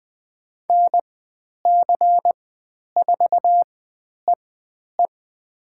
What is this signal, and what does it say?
Morse code, heard as a keyed tone, about 20 wpm, spaced as for 10 wpm, 700 Hz. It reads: NC4EE
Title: 109 Lesson 18g - Letter m - Call Signs 20wpm